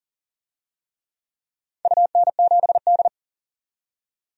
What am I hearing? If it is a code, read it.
Morse code, heard as a keyed tone, about 39 wpm, 700 Hz. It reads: UN7D
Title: DXCC-100 Call Signs - 4 Characters - 40wpm